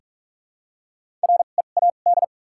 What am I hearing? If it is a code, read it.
Morse code, heard as a keyed tone, about 46 wpm, spaced as for 32 wpm, 700 Hz. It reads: READ
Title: 2x Character Spacing - Top 200 Words - 45wpm